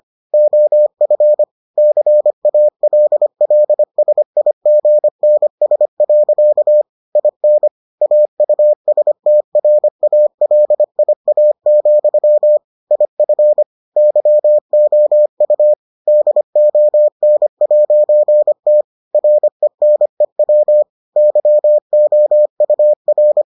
Morse code, heard as a keyed tone, about 25 wpm, 600 Hz. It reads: OF CALLSIGNS. IN AUSTRALIA, IF YOU DON'T RENEW YOUR